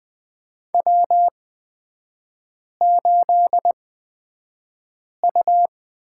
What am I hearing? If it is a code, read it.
Morse code, heard as a keyed tone, about 20 wpm, spaced as for 5 wpm, 700 Hz. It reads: W8U
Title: 236 Lesson 38e - Letter q - Three Characters 20wpm